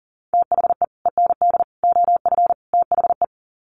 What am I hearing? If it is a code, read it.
Morse code, heard as a keyed tone, about 40 wpm, 700 Hz. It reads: THE END OF THE
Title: Sets of 4 Words 40wpm